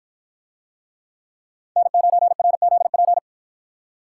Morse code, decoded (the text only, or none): N9RZP